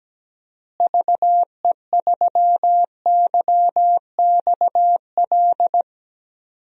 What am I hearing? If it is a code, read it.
Morse code, heard as a keyed tone, about 17 wpm, 700 Hz. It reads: VE3YXL
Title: DXCC-100 Call Signs - 6 Characters - Repeated 2x - 17wpm